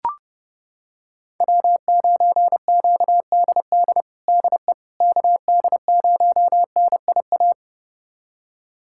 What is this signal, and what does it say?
Morse code, heard as a keyed tone, about 30 wpm, 700 Hz. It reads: W9QDD DE KD0NIA
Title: USA Callsign DE Callsign 30wpm